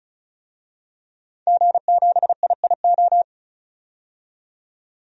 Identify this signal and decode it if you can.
Morse code, heard as a keyed tone, about 36 wpm, 700 Hz. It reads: G7IIO
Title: DXCC-100 Call Signs - 5 Characters - 35wpm